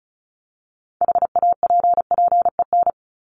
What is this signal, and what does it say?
Morse code, heard as a keyed tone, about 34 wpm, 700 Hz. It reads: HAPPEN
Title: ED Suffix - Top 100 Words with Base Words 35wpm